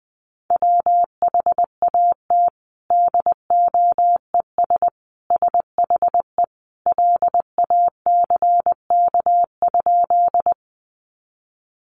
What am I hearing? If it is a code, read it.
Morse code, heard as a keyed tone, about 20 wpm, 700 Hz. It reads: WHAT DOES SHE LACK?
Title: Sentences from Top 1100 Words 20wpm